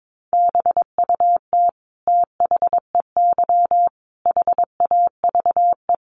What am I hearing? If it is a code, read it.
Morse code, heard as a keyed tone, about 22 wpm, 700 Hz. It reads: BUT THEY HAVE
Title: Sentences from Top 1200 Words 22wpm